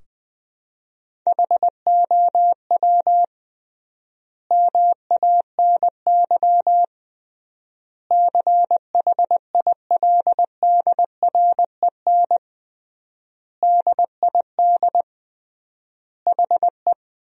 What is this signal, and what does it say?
Morse code, heard as a keyed tone, about 20 wpm, 700 Hz. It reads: HOW MANY CHILDREN DID HE
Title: 3x Word-Spacing - Sentences from Top 400 Words - 20wpm